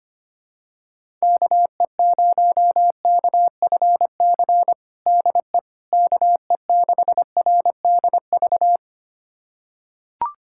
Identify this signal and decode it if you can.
Morse code, heard as a keyed tone, about 25 wpm, 700 Hz. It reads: KE0KFC DE KE6RDV
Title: USA Callsign DE Callsign - 6 Characters - 25wpm